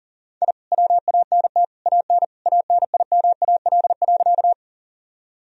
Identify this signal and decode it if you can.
Morse code, heard as a keyed tone, about 40 wpm, 700 Hz. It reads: I WANT AN ANIMAL.
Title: Sentences from Top 1000 Words - Review Part 1 - 40wpm